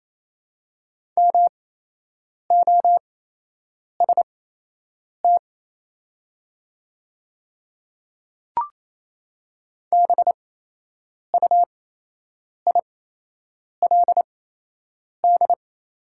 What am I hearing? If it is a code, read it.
Morse code, heard as a keyed tone, about 27 wpm, spaced as for 8 wpm, 690 Hz. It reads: MOST BUILD
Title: 8x Character Spacing - Top 200 Words - 28wpm